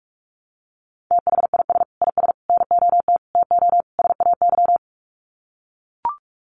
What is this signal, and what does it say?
Morse code, heard as a keyed tone, about 44 wpm, 700 Hz. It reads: THIS IS NOT TO SAY
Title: Sets of 5 Words Encore 45wpm